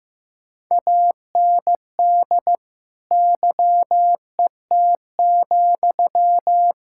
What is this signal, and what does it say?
Morse code, heard as a keyed tone, about 15 wpm, 700 Hz. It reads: AND YET,
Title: Sentences from Top 1600 Words 15wpm